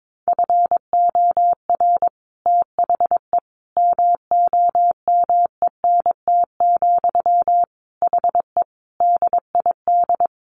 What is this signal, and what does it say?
Morse code, heard as a keyed tone, about 22 wpm, 700 Hz. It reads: FOR THE MOMENT, HE DID
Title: Sentences from Top 1400 Words 22wpm